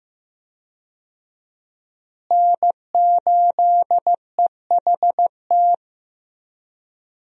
Morse code, heard as a keyed tone, about 15 wpm, 700 Hz. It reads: N8EHT
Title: USA Call Signs - 5 Characters - Repeated 3x - 15wpm